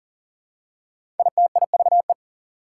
Morse code, heard as a keyed tone, about 42 wpm, 700 Hz. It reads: ITIVE
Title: Prefixes and Suffixes 40wpm